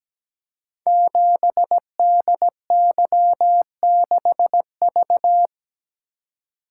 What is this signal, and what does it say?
Morse code, heard as a keyed tone, about 17 wpm, 690 Hz. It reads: 7DY6V
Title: Code Groups - 5 Letter:Numbers - 17wpm